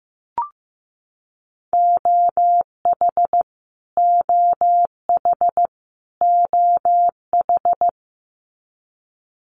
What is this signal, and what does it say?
Morse code, heard as a keyed tone, about 15 wpm, 700 Hz. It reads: OH OH OH
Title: Top 500 Words Repeated 15wpm